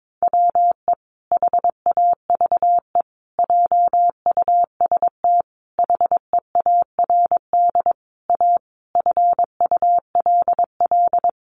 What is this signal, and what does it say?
Morse code, heard as a keyed tone, about 22 wpm, 700 Hz. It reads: WE HAVE JUST HEARD A FULL
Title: Sentences from Top 900 Words 22wpm